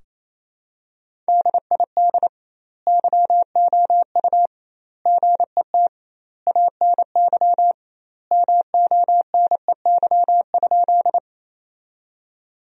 Morse code, heard as a keyed tone, about 28 wpm, 700 Hz. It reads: DID YOU GET ANY MONEY?
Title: 2x Word-Spacing - Sentences from Top 400 Words - 28wpm